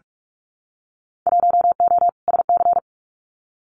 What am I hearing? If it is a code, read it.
Morse code, heard as a keyed tone, about 44 wpm, 700 Hz. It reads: 1O SC
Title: Winter Field Day Exchanges 45wpm